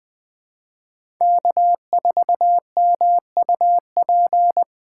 Morse code, heard as a keyed tone, about 20 wpm, 700 Hz. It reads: K4MUP
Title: USA Call Signs - 5 Characters - 2x Character Spacing - 20wpm